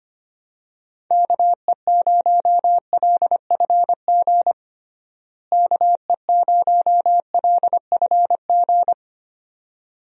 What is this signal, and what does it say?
Morse code, heard as a keyed tone, about 25 wpm, 700 Hz. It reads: KE0LFG KE0LFG
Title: USA Call Signs - 6 Characters - Repeated 2x - 25wpm